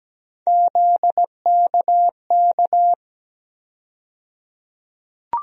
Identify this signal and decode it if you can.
Morse code, heard as a keyed tone, about 17 wpm, 700 Hz. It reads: ZKK
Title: USA Call Sign Postfixes - 3 Letters 17wpm